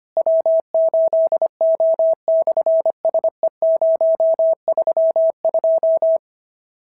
Morse code, heard as a keyed tone, about 25 wpm, 650 Hz. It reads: W8O/SE032
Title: SOTA QSO Practice 25wpm